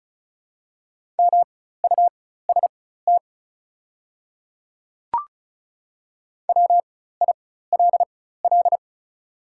Morse code, heard as a keyed tone, about 36 wpm, spaced as for 16 wpm, 700 Hz. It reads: MUST WILL
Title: 4x Character Spacing - Top 200 Words - 35wpm